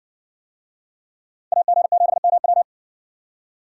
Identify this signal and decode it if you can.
Morse code, heard as a keyed tone, about 57 wpm, 700 Hz. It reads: AG7MW